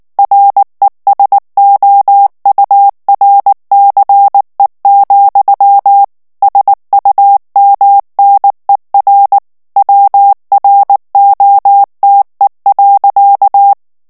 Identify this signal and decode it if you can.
Morse code, heard as a keyed tone, about 19 wpm, 800 Hz. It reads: RESOURCE, SUMNER WROTE.
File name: ANGLAIS texte clair 3 vitesse 18 mots minute